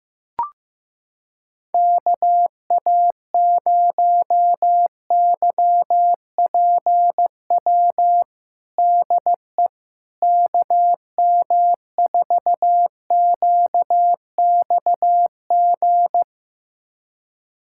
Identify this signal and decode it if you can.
Morse code, heard as a keyed tone, about 16 wpm, 700 Hz. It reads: KA0YPW DE KM4QXG
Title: USA Callsign DE Callsign - 6 Characters - 15wpm